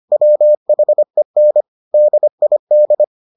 Morse code, heard as a keyed tone, about 25 wpm, 600 Hz. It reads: WHEN DID